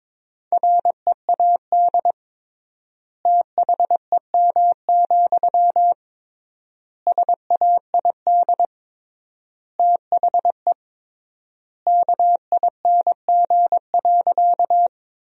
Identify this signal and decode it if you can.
Morse code, heard as a keyed tone, about 22 wpm, 700 Hz. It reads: READ THEM, SAID THE KING.
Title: 3x Word-Spacing - Sentences from Top 400 Words - 22wpm